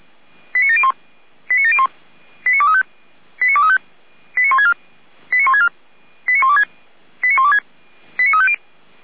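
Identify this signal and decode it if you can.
Five-tone selective call (ZVEI), heard as quick runs of five tones, tones of about 70 ms. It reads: 89701 89701 89235 89235 89164 89164 89126 89126 89350
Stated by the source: ZVEI five-tone selective call received on SDR.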